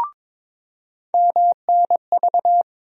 Morse code, heard as a keyed tone, about 21 wpm, 700 Hz. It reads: MNV
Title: USA Call Sign Postfixes - 3 Letters 22wpm